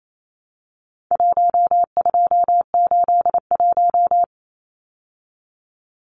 Morse code, heard as a keyed tone, about 28 wpm, 700 Hz. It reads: 1281
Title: Numbers - 4-Digit 28wpm